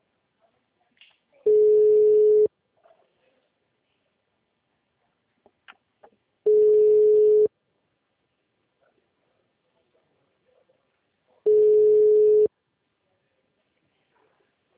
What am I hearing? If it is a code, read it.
Ringback tone (Europe, 425 Hz).